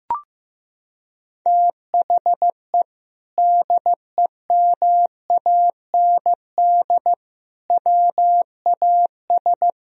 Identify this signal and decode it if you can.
Morse code, heard as a keyed tone, about 15 wpm, 700 Hz. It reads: THE DEMAND WAS
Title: Sentences from Top 1000 Words - Review Part 4 - 15wpm